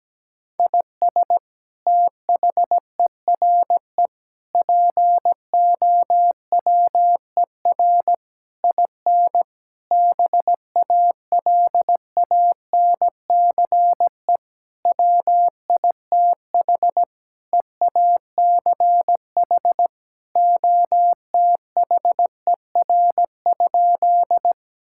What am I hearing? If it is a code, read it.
Morse code, heard as a keyed tone, about 17 wpm, 700 Hz. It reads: IS THERE POWER IN BALANCE WITH EACH OTHER?
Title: Sentences from Top 1000 Words - Review Part 6 - 17wpm